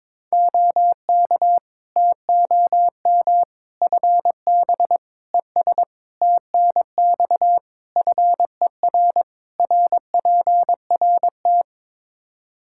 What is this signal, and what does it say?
Morse code, heard as a keyed tone, about 23 wpm, 700 Hz. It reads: OK TOM FB ES TNX FER RPRT
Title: Practice QSO - Exchange 2 - Info - 22wpm